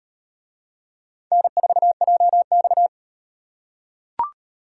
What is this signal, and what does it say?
Morse code, heard as a keyed tone, about 38 wpm, 700 Hz. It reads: N4JX